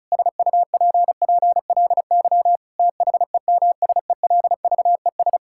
Morse code, heard as a keyed tone, about 34 wpm, 700 Hz. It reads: SUPPLY THEMSELVES
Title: Sentences from Top 1000 Words - Review Part 5 - 35wpm